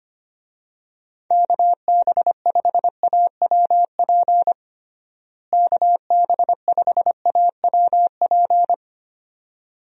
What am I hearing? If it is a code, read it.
Morse code, heard as a keyed tone, about 25 wpm, 700 Hz. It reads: KB5AWP KB5AWP
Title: USA Call Signs - 6 Characters - Repeated 2x - 25wpm